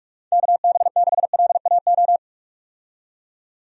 Morse code, heard as a keyed tone, about 46 wpm, 700 Hz. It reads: KB6LAO
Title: USA Call Signs - 6 Characters - 45wpm